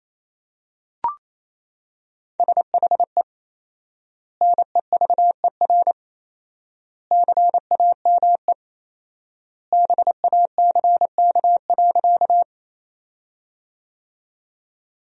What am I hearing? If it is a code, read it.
Morse code, heard as a keyed tone, about 28 wpm, 700 Hz. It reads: SHE NEVER CAME BACK.
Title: 4x Word-Spacing - Sentences from Top 300 Words - 28wpm